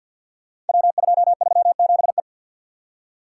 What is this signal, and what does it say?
Morse code, heard as a keyed tone, about 50 wpm, 700 Hz. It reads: W237E
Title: Code Groups - 5 Letter:Numbers - 50wpm